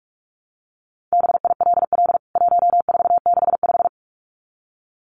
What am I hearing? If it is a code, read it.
Morse code, heard as a keyed tone, about 44 wpm, 700 Hz. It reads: BILL 1465
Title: CWT Exchange 45wpm